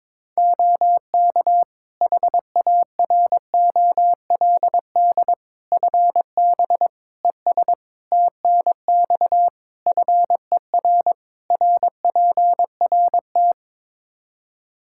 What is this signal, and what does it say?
Morse code, heard as a keyed tone, about 22 wpm, 700 Hz. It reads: OK HAROLD FB ES TNX FER RPRT
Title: Practice QSO - Exchange 2 - Info - 22wpm